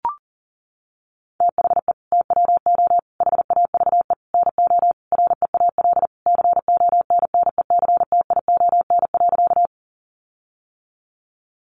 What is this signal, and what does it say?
Morse code, heard as a keyed tone, about 40 wpm, 700 Hz. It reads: THE TWO HAVE NO REAL CONNECTION.
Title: Sentences from Top 1300 Words 40wpm